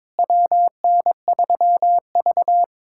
Morse code, heard as a keyed tone, about 22 wpm, 700 Hz. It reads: WN3V
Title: USA Callsign DE Callsign - 4 Characters - 22wpm